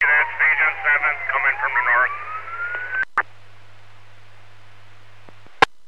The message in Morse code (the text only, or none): none